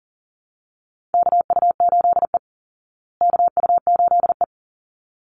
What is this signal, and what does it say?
Morse code, heard as a keyed tone, about 40 wpm, 700 Hz. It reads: KU8E KU8E
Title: Call Signs Repeated 3x - 40wpm